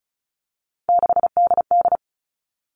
Morse code, heard as a keyed tone, about 34 wpm, 700 Hz. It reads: 6DD
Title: All Cut Numbers - 3-Digit 35wpm